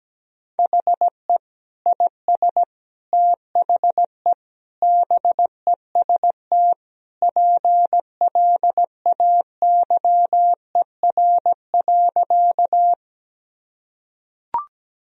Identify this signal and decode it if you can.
Morse code, heard as a keyed tone, about 17 wpm, 700 Hz. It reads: HE IS THE BEST PLAYER.
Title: Sentences from Top 1000 Words - Review Part 4 - 17wpm